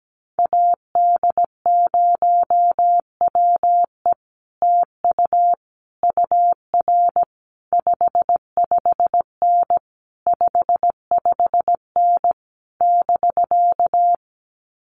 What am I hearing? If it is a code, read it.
Morse code, heard as a keyed tone, about 17 wpm, 700 Hz. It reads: AD0WE TU UR 55N 55N <BK>
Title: Common QSO Phrases Encore 17wpm